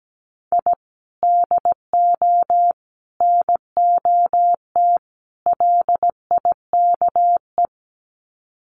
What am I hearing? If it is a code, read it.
Morse code, heard as a keyed tone, about 17 wpm, 700 Hz. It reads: I DO NOT LIKE
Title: Sets of 4 Words 17wpm